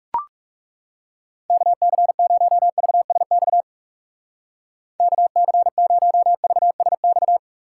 Morse code, heard as a keyed tone, about 37 wpm, 700 Hz. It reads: KC0VSX KC0VSX
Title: Speed Racing - USA 2x3 Call Signs 30wpm